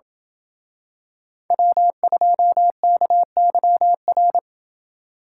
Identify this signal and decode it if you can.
Morse code, heard as a keyed tone, about 27 wpm, 700 Hz. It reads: W2KYR